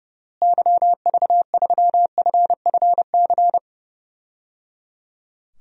Morse code, heard as a keyed tone, about 30 wpm, 700 Hz. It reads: YV3FFC
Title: DXCC-100 Call Signs - 6 Characters - Repeated 3x - 30wpm